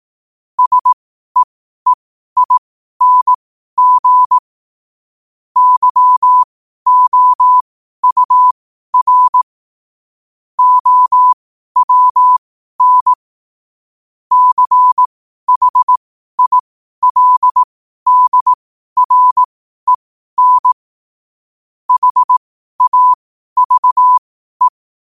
Morse code, heard as a keyed tone, about 18 wpm, spaced as for 12 wpm, 1000 Hz. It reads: SEEING YOUR OWN CHILDREN HAVE
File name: qotdfiles / 12wpm